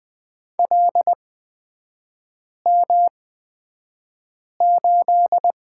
Morse code, heard as a keyed tone, about 20 wpm, spaced as for 5 wpm, 700 Hz. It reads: LM8
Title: 250 Lesson 40e - BK - Three Characters 20wpm